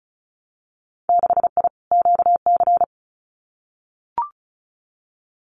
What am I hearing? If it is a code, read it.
Morse code, heard as a keyed tone, about 36 wpm, 700 Hz. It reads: 6I QC
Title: Winter Field Day Exchanges Encore 35wpm